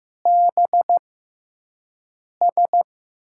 Morse code, heard as a keyed tone, about 15 wpm, spaced as for 5 wpm, 700 Hz. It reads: BS
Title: USA Call Signs - 5 Characters - 6x Character Spacing - 15wpm